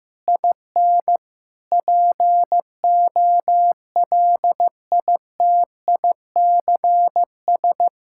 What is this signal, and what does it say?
Morse code, heard as a keyed tone, about 15 wpm, 700 Hz. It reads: IN POLITICS
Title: Sentences from Top 1000 Words - Review Part 6 - 15wpm